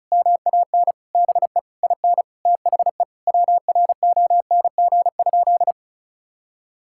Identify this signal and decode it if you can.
Morse code, heard as a keyed tone, about 34 wpm, 700 Hz. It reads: MAN BE IN THE WRONG?
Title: Sentences from Top 1000 Words - Review Part 6 - 35wpm